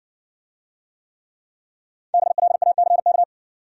Morse code, heard as a keyed tone, about 55 wpm, 700 Hz. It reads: BLACK